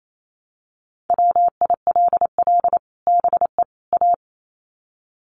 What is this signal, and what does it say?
Morse code, heard as a keyed tone, about 28 wpm, 700 Hz. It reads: WILL BE A
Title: Sets of 3 Words 28wpm